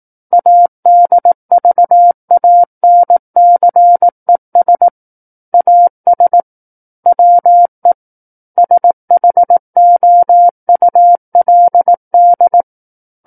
Morse code, heard as a keyed tone, about 18 wpm, 700 Hz. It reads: ADVANCES AS WE SHOULD
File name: Book One / 09-Chapter 9 - 18 WPM